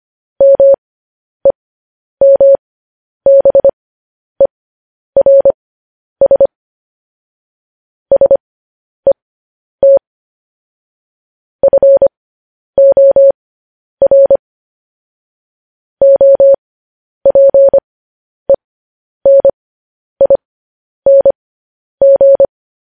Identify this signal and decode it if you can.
Morse code, heard as a keyed tone, about 25 wpm, spaced as for 10 wpm, 550 Hz. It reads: MEMBERS SET FOR OPENING